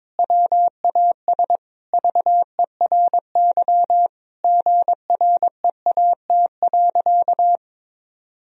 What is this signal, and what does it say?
Morse code, heard as a keyed tone, about 21 wpm, 700 Hz. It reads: WAS VERY GREAT.
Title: Sentences from Top 1000 Words - Review Part 1 - 22wpm